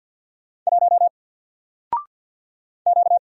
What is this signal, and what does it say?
Morse code, heard as a keyed tone, about 48 wpm, 700 Hz. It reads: 1 Q
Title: Single Character (Letter:Number:Punctuation:Prosign) 50wpm